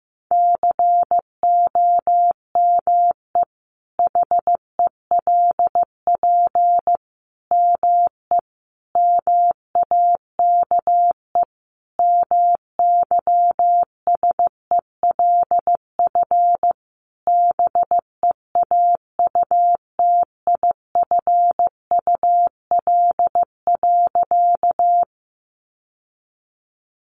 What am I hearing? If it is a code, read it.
Morse code, heard as a keyed tone, about 15 wpm, 700 Hz. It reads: COME HELP ME MAKE MYSELF BEAUTIFUL.
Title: Sentences from Top 1300 Words 15wpm